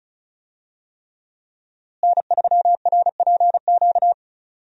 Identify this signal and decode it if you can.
Morse code, heard as a keyed tone, about 35 wpm, 700 Hz. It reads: N3RPQ
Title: USA Call Signs - 5 Characters - Repeated 3x - 35wpm